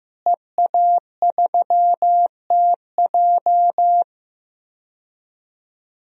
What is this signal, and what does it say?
Morse code, heard as a keyed tone, about 16 wpm, 700 Hz. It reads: EA3TJ
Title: DXCC-100 Call Signs - 5 Characters - Repeated 3x - 15wpm